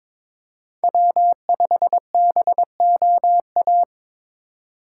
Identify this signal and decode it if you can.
Morse code, heard as a keyed tone, about 26 wpm, 700 Hz. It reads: W5BOA
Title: Speed Racing - USA 1x3 Call Signs 22wpm